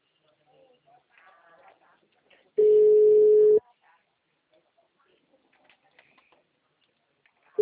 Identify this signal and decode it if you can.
Ringback tone (Europe, 425 Hz).